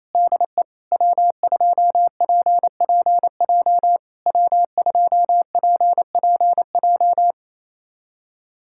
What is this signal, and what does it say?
Morse code, heard as a keyed tone, about 28 wpm, 700 Hz. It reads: DE W2PPJ W2PPJ
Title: SOTA QSO Practice 28wpm